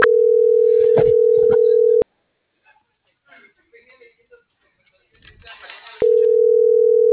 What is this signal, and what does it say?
Ringback tone (North America), 440 + 480 Hz.